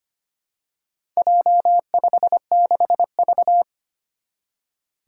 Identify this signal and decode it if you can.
Morse code, heard as a keyed tone, about 25 wpm, 700 Hz. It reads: J56V
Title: Four Challenging Letter-Number Combinations 25wpm